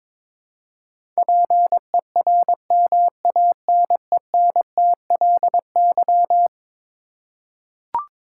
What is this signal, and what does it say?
Morse code, heard as a keyed tone, about 22 wpm, 700 Hz. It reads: PERMANENTLY
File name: practice / 22wpm / Eleven Letter Words - 22wpm